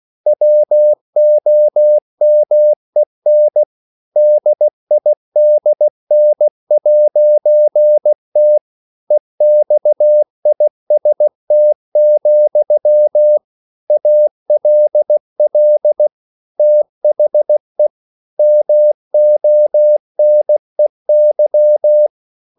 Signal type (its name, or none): Morse code